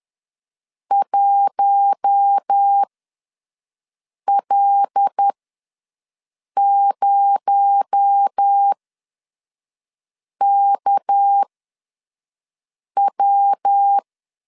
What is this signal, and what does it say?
Morse code, heard as a keyed tone, about 10 wpm, spaced as for 5 wpm, 800 Hz. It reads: 1L0KW
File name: LEZIONI / 6 PKX0 / 6 PKX0 ws 10wpm 4220